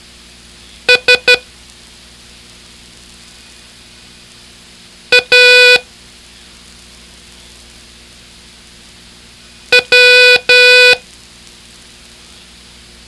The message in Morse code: S A W